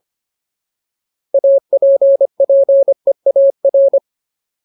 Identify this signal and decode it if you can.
Morse code, heard as a keyed tone, about 25 wpm, 550 Hz. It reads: APPEAR
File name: Ultimate Word List / 25WPM Ultimate